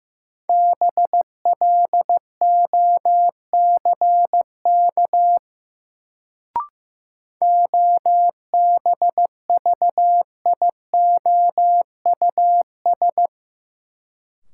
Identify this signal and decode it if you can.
Morse code, heard as a keyed tone, about 15 wpm, 700 Hz. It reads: BLOCK OBVIOUS
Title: Sentences from Top 1600 Words - New Word Drill Alt - 15wpm